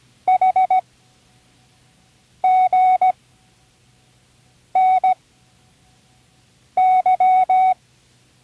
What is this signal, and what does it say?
Morse code, heard as a keyed tone, about 17 wpm, spaced as for 5 wpm, 750 Hz. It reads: HGNY
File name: CWResources / Learning Morse Code (from Tune in the World) / 24 random